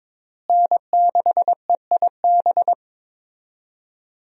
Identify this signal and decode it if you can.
Morse code, heard as a keyed tone, about 22 wpm, 700 Hz. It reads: N6EIB
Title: USA Call Signs - 5 Characters - Repeated 3x - 22wpm